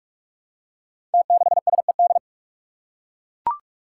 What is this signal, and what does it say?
Morse code, heard as a keyed tone, about 44 wpm, 700 Hz. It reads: T6SED
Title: Five Challenging Letter-Number Combinations 45wpm